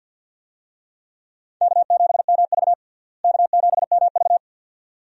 Morse code, heard as a keyed tone, about 50 wpm, 700 Hz. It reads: K7MV K7MV
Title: USA Call Signs - 4 Characters - Repeated 2x - 50wpm